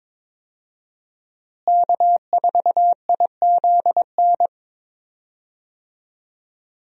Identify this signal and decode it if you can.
Morse code, heard as a keyed tone, about 22 wpm, 700 Hz. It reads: K4IZN
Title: USA Call Signs - 5 Characters - Repeated 2x - 22wpm